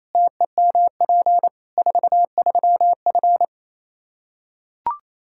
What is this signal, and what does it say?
Morse code, heard as a keyed tone, about 28 wpm, 700 Hz. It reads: TEMP 43F
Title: K1USN SST QSO Practice 28wpm